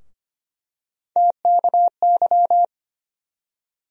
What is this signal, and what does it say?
Morse code, heard as a keyed tone, about 25 wpm, 700 Hz. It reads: TKY
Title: USA Call Sign Postfixes - 3 Letters 25wpm